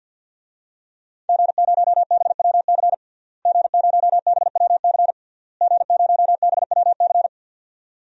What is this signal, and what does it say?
Morse code, heard as a keyed tone, about 50 wpm, 700 Hz. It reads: G0BWC G0BWC G0BWC
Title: DXCC-100 Call Signs - 5 Characters - Repeated 3x - 50wpm